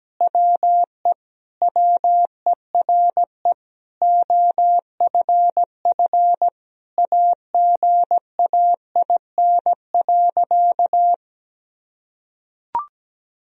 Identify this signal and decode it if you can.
Morse code, heard as a keyed tone, about 17 wpm, 700 Hz. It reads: WE WERE OFF AGAIN.
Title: Sentences from Top 1000 Words - Review Part 2 - 17wpm